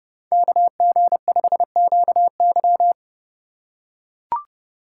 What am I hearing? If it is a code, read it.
Morse code, heard as a keyed tone, about 32 wpm, 700 Hz. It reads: KG5QY
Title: USA Call Signs - 5 Characters - 3x Character Spacing - 30wpm